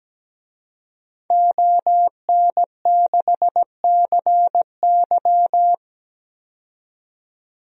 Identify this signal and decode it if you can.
Morse code, heard as a keyed tone, about 17 wpm, 700 Hz. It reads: ON6CY
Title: DXCC-100 Call Signs - 5 Characters - 17wpm